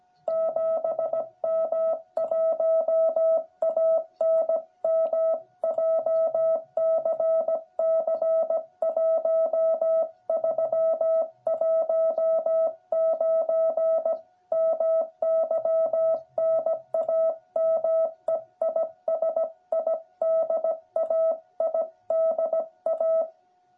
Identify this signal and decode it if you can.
Morse code, heard as a keyed tone, about 17 wpm, 660 Hz. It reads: 7M1ANM JCC 1319 MY NAME IS IDA IDA
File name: jh8cbh / 2021cw